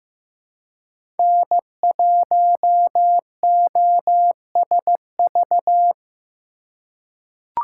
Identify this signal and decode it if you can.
Morse code, heard as a keyed tone, about 15 wpm, 700 Hz. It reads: N1OSV
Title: USA Call Signs - 5 Characters - 3x Character Spacing - 15wpm